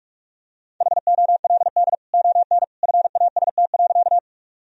Morse code, heard as a keyed tone, about 46 wpm, 700 Hz. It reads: HOLD ON FAST.
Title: Sentences from Top 1000 Words - Review Part 2 - 45wpm